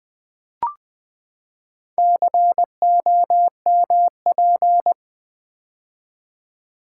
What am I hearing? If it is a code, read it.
Morse code, heard as a keyed tone, about 20 wpm, 700 Hz. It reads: COMP
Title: Build A Word - 4 Letters At A Time - 10-Letter Words - 20wpm